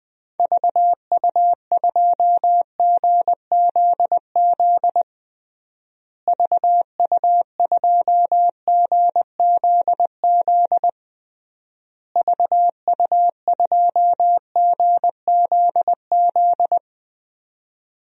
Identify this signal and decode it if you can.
Morse code, heard as a keyed tone, about 21 wpm, 700 Hz. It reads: VU2GZZ VU2GZZ VU2GZZ
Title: DXCC-100 Call Signs - 6 Characters - Repeated 3x - 20wpm